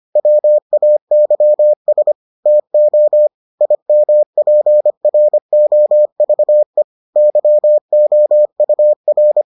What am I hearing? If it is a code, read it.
Morse code, heard as a keyed tone, about 25 wpm, 600 Hz. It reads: WAYS TO IMPROVE YOUR